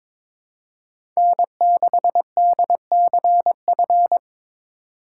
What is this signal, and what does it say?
Morse code, heard as a keyed tone, about 21 wpm, 700 Hz. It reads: N6DCF
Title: USA Call Signs - 5 Characters - 2x Character Spacing - 22wpm